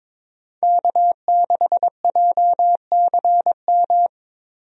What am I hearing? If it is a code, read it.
Morse code, heard as a keyed tone, about 22 wpm, 700 Hz. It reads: K6JCM
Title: USA Call Signs - 5 Characters - Repeated 2x - 22wpm